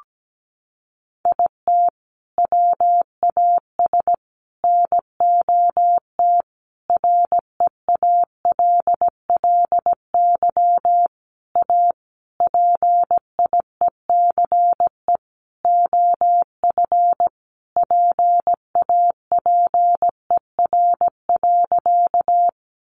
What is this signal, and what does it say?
Morse code, heard as a keyed tone, about 17 wpm, 700 Hz. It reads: IT WAS NOT REALLY A PIECE OF PAPER.
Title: Sentences from Top 600 Words 17wpm